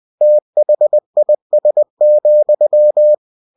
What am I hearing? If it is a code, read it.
Morse code, heard as a keyed tone, about 20 wpm, 600 Hz. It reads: THIS,
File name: cw-books / winnie-the-pooh / winnie-20wpm-page-0083